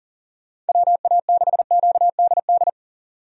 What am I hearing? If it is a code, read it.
Morse code, heard as a keyed tone, about 40 wpm, 700 Hz. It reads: WA6QDD
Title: USA Call Signs - 6 Characters - 40wpm